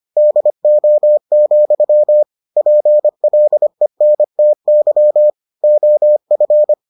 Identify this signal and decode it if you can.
Morse code, heard as a keyed tone, about 25 wpm, 600 Hz. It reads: DO, PLENTY OF